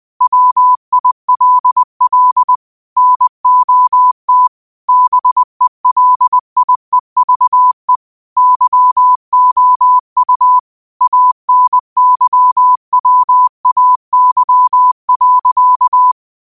Morse code, heard as a keyed tone, about 20 wpm, 1000 Hz. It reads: WILL NOT BELIEVE YOU ANYWAY.